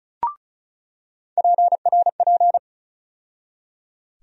Morse code, heard as a keyed tone, about 36 wpm, 700 Hz. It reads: PRP
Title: USA Call Sign Postfixes - 3 Letters 35wpm